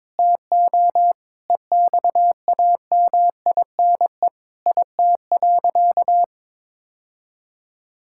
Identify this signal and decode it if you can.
Morse code, heard as a keyed tone, about 22 wpm, 700 Hz. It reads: TO EXAMINE IT.
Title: Sentences from Top 1600 Words 22wpm